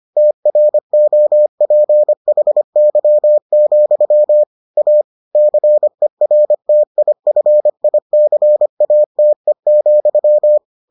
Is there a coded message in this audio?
Yes: Morse code from 0.2 to 10.6 s, heard as a keyed tone, about 25 wpm, 600 Hz. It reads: TROPHY, A CERTIFICATE,